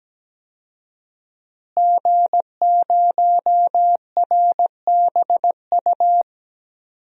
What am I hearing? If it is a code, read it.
Morse code, heard as a keyed tone, about 17 wpm, 700 Hz. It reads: G0RBU
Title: DXCC-100 Call Signs - 5 Characters - Repeated 3x - 17wpm